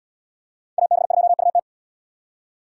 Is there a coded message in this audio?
Yes: Morse code from 0.8 to 1.6 s, heard as a keyed tone, about 67 wpm, 700 Hz. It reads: AB2UT